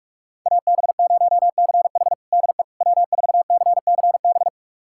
Morse code, heard as a keyed tone, about 44 wpm, 700 Hz. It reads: AB0CH DE W4CCB
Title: USA Callsign DE Callsign - 5 Characters - 45wpm